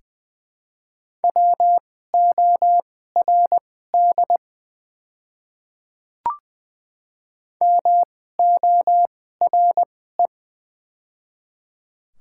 Morse code, heard as a keyed tone, about 20 wpm, spaced as for 14 wpm, 700 Hz. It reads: WORD MORE
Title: 2x Character Spacing - Top 100 Words - 20wpm